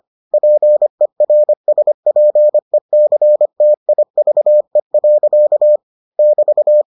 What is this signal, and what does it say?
Morse code, heard as a keyed tone, about 25 wpm, 600 Hz. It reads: PERSPECTIVE. <BT>